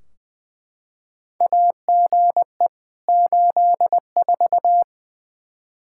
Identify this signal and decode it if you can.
Morse code, heard as a keyed tone, about 21 wpm, 700 Hz. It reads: AGE 84
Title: K1USN SST QSO Practice 20wpm